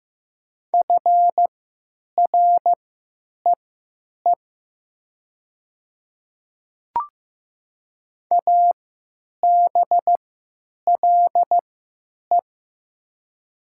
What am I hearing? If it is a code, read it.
Morse code, heard as a keyed tone, about 15 wpm, spaced as for 8 wpm, 700 Hz. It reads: FREE ABLE
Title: 3x Character Spacing - Top 500 Words - 15wpm